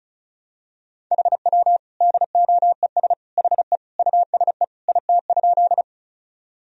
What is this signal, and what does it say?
Morse code, heard as a keyed tone, about 35 wpm, 700 Hz. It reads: HW DOES HE USE IT?
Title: Sentences from Top 200 Words and Abbreviations 35wpm